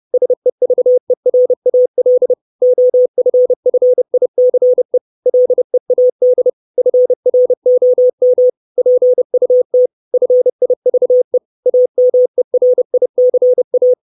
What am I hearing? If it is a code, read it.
Morse code, heard as a keyed tone, about 30 wpm, 500 Hz. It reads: SEVERAL OFFICE LEAD FROM PUT FIVE AMERICA
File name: 632 Words 30WPM